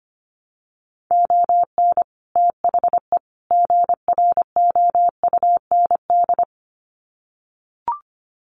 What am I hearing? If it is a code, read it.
Morse code, heard as a keyed tone, about 25 wpm, 700 Hz. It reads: ON THE GROUND
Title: Sets of 3 Words 25wpm